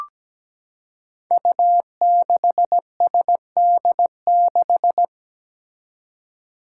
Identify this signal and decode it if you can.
Morse code, heard as a keyed tone, about 17 wpm, 700 Hz. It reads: U6SD6
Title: Five Challenging Letter-Number Combinations 17wpm